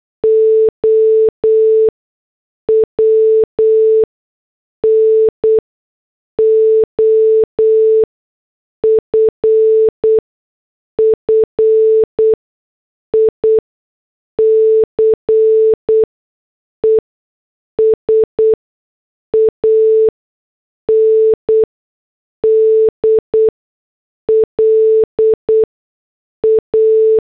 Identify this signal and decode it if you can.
Morse code, heard as a keyed tone, about 8 wpm, spaced as for 6 wpm, 440 Hz. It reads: OWN OFFICES AND LA
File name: morse / headlines